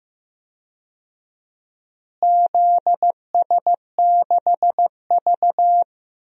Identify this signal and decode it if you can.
Morse code, heard as a keyed tone, about 15 wpm, 700 Hz. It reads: ZS6V
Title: DXCC-100 Call Signs - 4 Characters - Repeated 2x - 15wpm